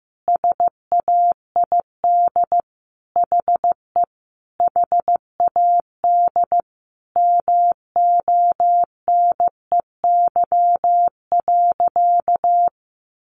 Morse code, heard as a keyed tone, about 14 wpm, 700 Hz. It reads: SAID HE HAD MONEY.
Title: Sentences from Top 500 Words 15wpm